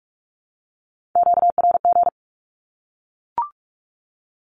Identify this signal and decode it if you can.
Morse code, heard as a keyed tone, about 44 wpm, 700 Hz. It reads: QRG
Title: Q-Codes 45wpm